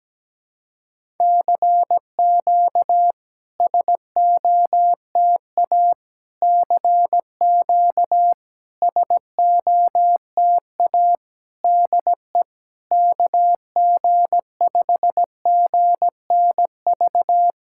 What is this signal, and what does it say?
Morse code, heard as a keyed tone, about 17 wpm, 690 Hz. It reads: CQ SOTA CQ SOTA DE KG5GNV
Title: SOTA QSO Practice 17wpm